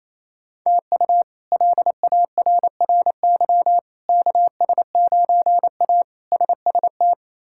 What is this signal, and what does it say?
Morse code, heard as a keyed tone, about 28 wpm, 700 Hz. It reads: TU LARRY KS9A SST
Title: K1USN SST QSO Practice - 28wpm